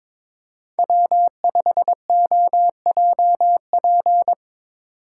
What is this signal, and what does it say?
Morse code, heard as a keyed tone, about 22 wpm, 700 Hz. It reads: W5OJP
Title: USA Call Signs - 5 Characters - 3x Character Spacing - 22wpm